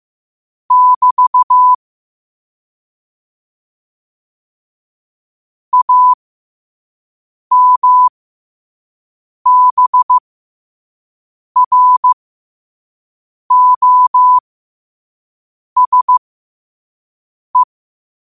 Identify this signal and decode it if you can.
Morse code, heard as a keyed tone, about 15 wpm, spaced as for 5 wpm, 1000 Hz. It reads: <BT> AMBROSE
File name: qotdfiles / 5wpm